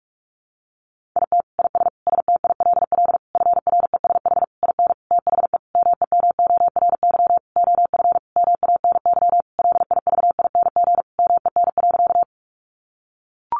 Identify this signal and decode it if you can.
Morse code, heard as a keyed tone, about 44 wpm, 700 Hz. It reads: IT IS STILL FRESH IN THE MEMORY OF MANY LIVING MEN.
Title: Sentences from Top 1300 Words 45wpm